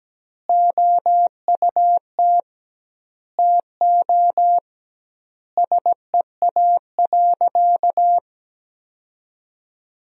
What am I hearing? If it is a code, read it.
Morse code, heard as a keyed tone, about 17 wpm, 700 Hz. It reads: OUT TO SEA.
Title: 2x Word-Spacing - Sentences from Top 400 Words - 17wpm